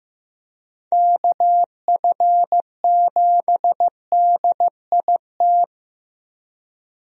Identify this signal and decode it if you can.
Morse code, heard as a keyed tone, about 15 wpm, 700 Hz. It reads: KF7DIT
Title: USA Call Signs - 6 Characters - Repeated 3x - 15wpm